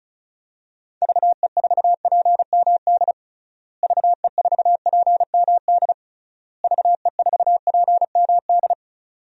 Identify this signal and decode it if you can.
Morse code, heard as a keyed tone, about 34 wpm, 700 Hz. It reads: VE4PMD VE4PMD VE4PMD
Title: DXCC-100 Call Signs - 6 Characters - Repeated 3x - 35wpm